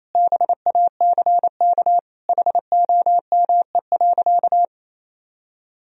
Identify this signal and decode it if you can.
Morse code, heard as a keyed tone, about 28 wpm, 700 Hz. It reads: BACK HOME.
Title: Sentences from Top 1000 Words - Review Part 3 - 28wpm